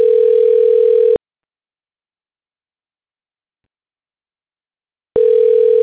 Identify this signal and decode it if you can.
Ringback tone (North America), 440 + 480 Hz.